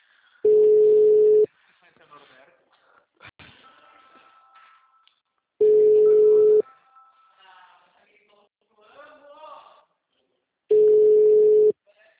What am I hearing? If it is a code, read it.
Ringback tone (Europe, 425 Hz), with speech.